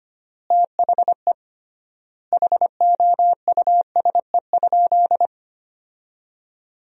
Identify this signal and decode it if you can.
Morse code, heard as a keyed tone, about 25 wpm, 700 Hz. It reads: THE HOUSE?
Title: 3x Word-Spacing - Sentences from Top 200 Words - 25wpm